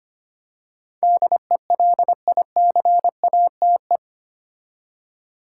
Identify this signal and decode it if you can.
Morse code, heard as a keyed tone, about 25 wpm, 700 Hz. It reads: DELICATE
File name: 8 Letter Words - First 3 Letters Given - 25wpm